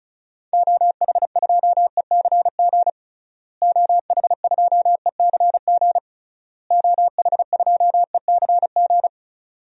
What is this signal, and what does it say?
Morse code, heard as a keyed tone, about 36 wpm, 700 Hz. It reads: OH2ECG OH2ECG OH2ECG
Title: DXCC-100 Call Signs - 6 Characters - Repeated 3x - 35wpm